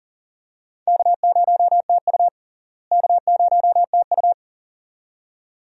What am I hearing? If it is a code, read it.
Morse code, heard as a keyed tone, about 38 wpm, 700 Hz. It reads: K0TU K0TU
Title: USA Call Signs - 4 Characters - Repeated 2x - 40wpm